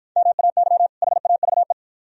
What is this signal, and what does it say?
Morse code, heard as a keyed tone, about 53 wpm, 700 Hz. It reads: MAY HAVE